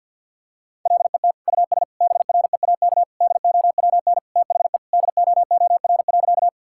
Morse code, heard as a keyed tone, about 47 wpm, 700 Hz. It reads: LET US BREAK DOWN THE DOOR.
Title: Sentences from Top 1000 Words - Review Part 6 - 50wpm